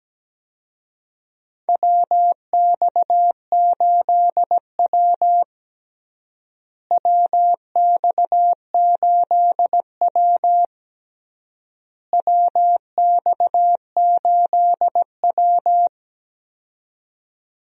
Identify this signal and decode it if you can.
Morse code, heard as a keyed tone, about 17 wpm, 700 Hz. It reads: WX8W WX8W WX8W
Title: USA Call Signs - 4 Characters - Repeated 3x - 17wpm